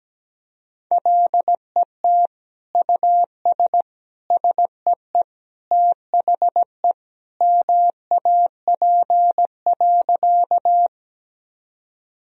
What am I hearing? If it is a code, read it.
Morse code, heard as a keyed tone, about 17 wpm, 700 Hz. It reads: LET US SEE THE MAP.
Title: Sentences from Top 1000 Words - Review Part 2 - 17wpm